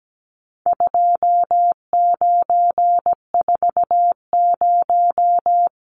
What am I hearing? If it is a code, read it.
Morse code, heard as a keyed tone, about 17 wpm, 700 Hz. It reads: 2940
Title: Numbers - 4-Digit 17wpm